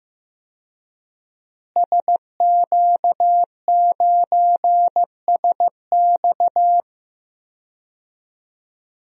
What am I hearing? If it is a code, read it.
Morse code, heard as a keyed tone, about 15 wpm, 700 Hz. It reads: SQ9SX
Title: DXCC-100 Call Signs - 5 Characters - Repeated 3x - 15wpm